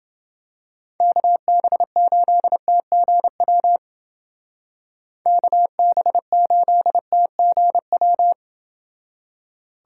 Morse code, heard as a keyed tone, about 28 wpm, 700 Hz. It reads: KB8TGW KB8TGW